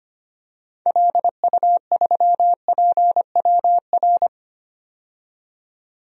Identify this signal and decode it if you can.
Morse code, heard as a keyed tone, about 25 wpm, 700 Hz. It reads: LU3PWR
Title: DXCC-100 Call Signs - 6 Characters - 25wpm